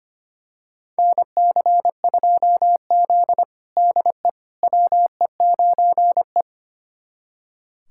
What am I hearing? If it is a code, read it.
Morse code, heard as a keyed tone, about 25 wpm, 700 Hz. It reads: NC2Z DE WE9E
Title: USA Callsign DE Callsign - 4 Characters - 25wpm